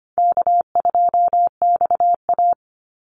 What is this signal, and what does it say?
Morse code, heard as a keyed tone, about 25 wpm, 700 Hz. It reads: K2XA
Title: Call Signs - 25wpm